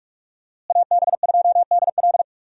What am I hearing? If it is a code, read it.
Morse code, heard as a keyed tone, about 46 wpm, 700 Hz. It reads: AB2DL
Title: USA Call Signs - 5 Characters - Repeated 2x - 45wpm